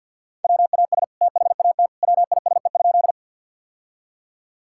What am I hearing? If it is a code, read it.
Morse code, heard as a keyed tone, about 50 wpm, 700 Hz. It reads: WAS THAT WISE?
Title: Sentences from Top 1600 Words 50wpm